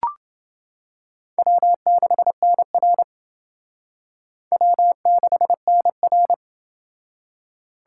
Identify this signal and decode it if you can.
Morse code, heard as a keyed tone, about 28 wpm, 700 Hz. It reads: W6NR W6NR